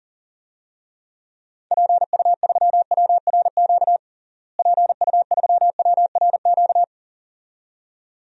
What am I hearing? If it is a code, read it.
Morse code, heard as a keyed tone, about 40 wpm, 700 Hz. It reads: PU3WRQ PU3WRQ
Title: DXCC-100 Call Signs - 6 Characters - Repeated 2x - 40wpm